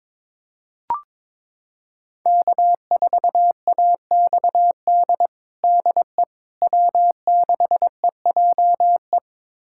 Morse code, heard as a keyed tone, about 22 wpm, 700 Hz. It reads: K4AXD DE W6EJE
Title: USA Callsign DE Callsign 22wpm